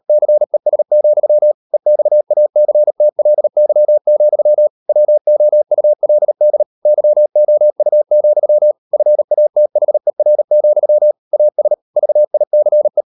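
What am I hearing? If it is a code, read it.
Morse code, heard as a keyed tone, about 38 wpm, 600 Hz. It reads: CES, EXACTLY, WOULD YOU, FATHER, AS VICE
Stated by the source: Morse code practice at 35–39 wpm sending English text of Ken Follett's novel Never, from CW med Gnister.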